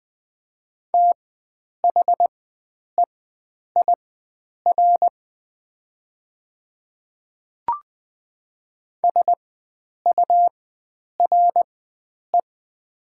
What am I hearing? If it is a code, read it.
Morse code, heard as a keyed tone, about 20 wpm, spaced as for 9 wpm, 700 Hz. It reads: THEIR SURE
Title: 4x Character Spacing - Top 300 Words - 20wpm